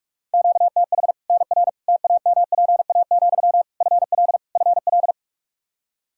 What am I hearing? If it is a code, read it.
Morse code, heard as a keyed tone, about 41 wpm, 700 Hz. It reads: QTH NR TAMPA, FL FL
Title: K1USN SST QSO Practice 45wpm